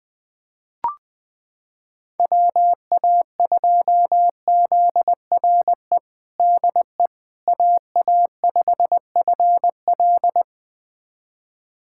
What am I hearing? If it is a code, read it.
Morse code, heard as a keyed tone, about 20 wpm, 700 Hz. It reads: WA2ZRE DE AA5FL
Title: USA Callsign DE Callsign 20wpm